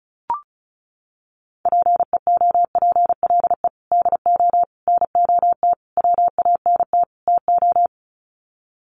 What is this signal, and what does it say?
Morse code, heard as a keyed tone, about 34 wpm, 700 Hz. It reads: PEOPLE DO NOT WANT TO
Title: Sets of 5 Words Encore2 35wpm